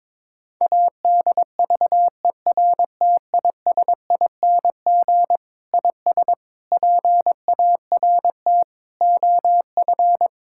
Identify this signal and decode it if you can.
Morse code, heard as a keyed tone, about 22 wpm, 700 Hz. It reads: ADVERTISING IS PART OF
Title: Sentences from Top 1000 Words - Review Part 6 - 22wpm